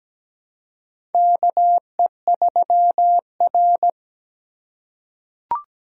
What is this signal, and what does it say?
Morse code, heard as a keyed tone, about 17 wpm, 700 Hz. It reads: KE3R
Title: USA Call Signs - 4 Characters - 3x Character Spacing - 17wpm